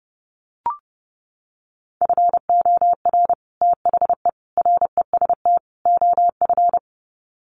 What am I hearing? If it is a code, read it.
Morse code, heard as a keyed tone, about 30 wpm, 700 Hz. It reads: FOR THE REST OF
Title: Sets of 4 Words 30wpm